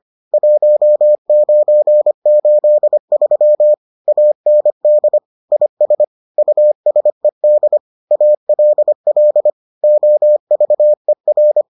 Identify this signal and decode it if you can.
Morse code, heard as a keyed tone, about 25 wpm, 600 Hz. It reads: 1983 AND IS USED ALL OVER